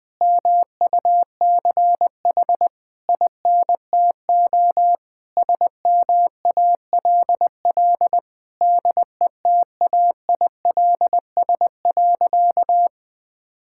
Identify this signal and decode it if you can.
Morse code, heard as a keyed tone, about 20 wpm, 700 Hz. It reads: MUCH INTO SMALL DETAILS.
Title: Sentences with 9-Word Phrases - 20wpm